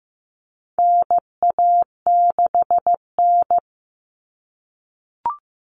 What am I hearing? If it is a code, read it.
Morse code, heard as a keyed tone, about 15 wpm, 700 Hz. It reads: NA6N
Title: Common Cut Numbers - 4-Digit 15wpm